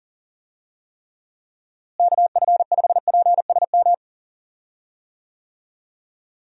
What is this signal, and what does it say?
Morse code, heard as a keyed tone, about 40 wpm, 700 Hz. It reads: KF5PSM
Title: USA Call Signs - 6 Characters - Repeated 2x - 40wpm